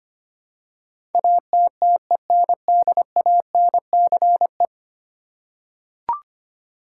Morse code, heard as a keyed tone, about 25 wpm, 700 Hz. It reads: ATTENDANCE
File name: 10 Letter Words - First 3 Letters Given - 25wpm